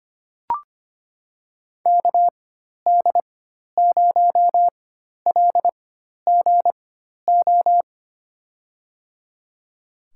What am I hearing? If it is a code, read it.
Morse code, heard as a keyed tone, about 25 wpm, spaced as for 12 wpm, 700 Hz. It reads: KD0LGO
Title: USA Call Signs - 6 Characters - 4x Character Spacing - 25wpm